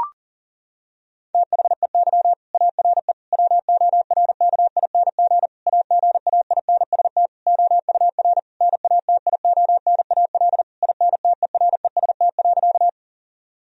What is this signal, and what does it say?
Morse code, heard as a keyed tone, about 37 wpm, 700 Hz. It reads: THEY ARE WORKING AGAINST OUR NATIONAL INTEREST.
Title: Sentences from Top 1000 Words - Review Part 4 - 40wpm